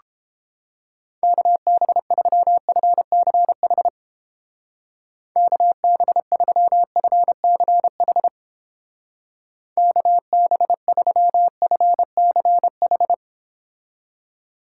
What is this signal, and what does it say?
Morse code, heard as a keyed tone, about 31 wpm, 700 Hz. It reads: KB3FCH KB3FCH KB3FCH